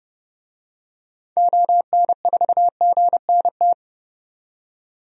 Morse code, heard as a keyed tone, about 30 wpm, 700 Hz. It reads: ON4GNT
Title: DXCC-100 Call Signs - 6 Characters - 30wpm